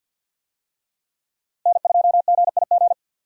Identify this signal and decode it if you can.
Morse code, heard as a keyed tone, about 52 wpm, 700 Hz. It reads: N2GIG
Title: USA Call Signs - 5 Characters - Repeated 3x - 50wpm